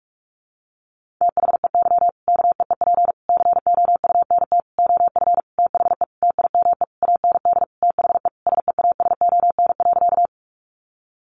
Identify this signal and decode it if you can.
Morse code, heard as a keyed tone, about 44 wpm, 700 Hz. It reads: THEY KEEP COUNT OF THE TIME AND THE SEASON.
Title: Sentences from Top 1500 Words 45wpm